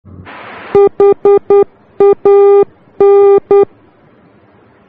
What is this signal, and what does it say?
Morse code, heard as a keyed tone, about 10 wpm, 400 Hz. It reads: HAN
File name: germany / mx / HAN-D